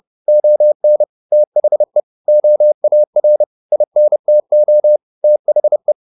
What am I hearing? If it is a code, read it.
Morse code, heard as a keyed tone, about 30 wpm, 600 Hz. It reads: ON THE OAR INTO THE